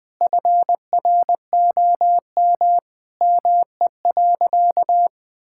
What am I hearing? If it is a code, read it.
Morse code, heard as a keyed tone, about 20 wpm, 700 Hz. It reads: FROM ME.
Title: Sentences from Top 1000 Words - Review Part 5 - 20wpm